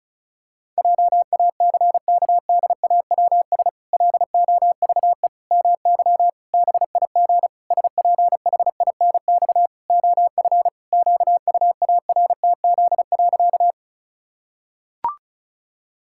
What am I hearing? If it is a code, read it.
Morse code, heard as a keyed tone, about 34 wpm, 700 Hz. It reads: JACKDAWS LOVE MY BIG SPHINX OF QUARTZ.
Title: Pangram Challenge v2 - 35wpm